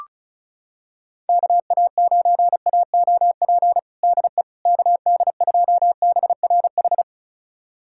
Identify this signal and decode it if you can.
Morse code, heard as a keyed tone, about 34 wpm, 700 Hz. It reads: KA9AOP DE KD2BRH
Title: USA Callsign DE Callsign - 6 Characters - 35wpm